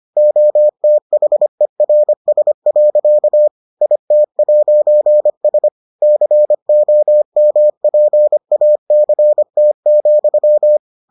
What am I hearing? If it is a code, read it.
Morse code, heard as a keyed tone, about 25 wpm, 600 Hz. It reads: OTHERS. IT'S COMPACT,